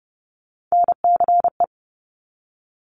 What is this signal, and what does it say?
Morse code, heard as a keyed tone, about 30 wpm, 700 Hz. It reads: NCE
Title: Top Three Letter Combinations in Words 30wpm